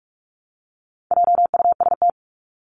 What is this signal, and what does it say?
Morse code, heard as a keyed tone, about 44 wpm, 700 Hz. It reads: JUST
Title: Top 200 Words Repeated 45wpm